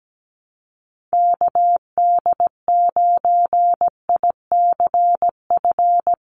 Morse code, heard as a keyed tone, about 17 wpm, 700 Hz. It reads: KD9ICF
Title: USA Call Signs 17wpm